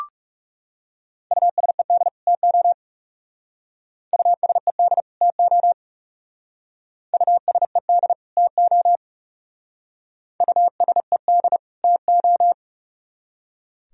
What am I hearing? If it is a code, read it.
Morse code, heard as a keyed tone, about 36 wpm, 700 Hz. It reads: USED TO USED TO USED TO USED TO
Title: Speed-Racing - Sets of 2 Words 30wpm